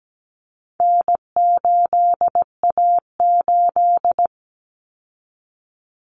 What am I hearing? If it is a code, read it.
Morse code, heard as a keyed tone, about 17 wpm, 700 Hz. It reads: N8A8
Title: Common Cut Numbers - 4-Digit 17wpm